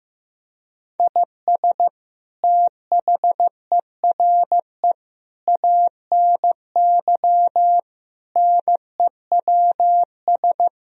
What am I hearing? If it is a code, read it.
Morse code, heard as a keyed tone, about 16 wpm, 700 Hz. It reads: IS THERE ANY NEWS
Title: Sentences from Top 1000 Words - Review Part 4 - 15wpm